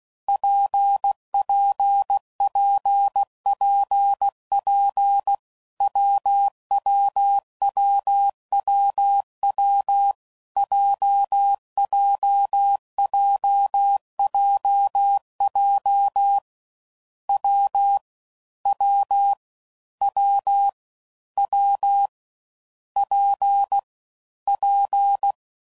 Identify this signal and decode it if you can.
Morse code, heard as a keyed tone, about 16 wpm, 800 Hz. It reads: PPPPP WWWWW JJJJJ W W W W P P
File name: Combinations / Difficult Combos / WPJ WPJ1